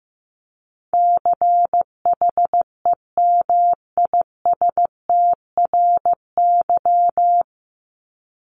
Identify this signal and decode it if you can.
Morse code, heard as a keyed tone, about 15 wpm, 700 Hz. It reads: CHEMISTRY
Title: Top 751-1000 Words 15wpm